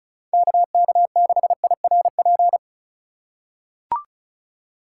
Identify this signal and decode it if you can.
Morse code, heard as a keyed tone, about 34 wpm, 700 Hz. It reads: KK6IRP
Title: USA Call Signs - 6 Characters - 2x Character Spacing - 35wpm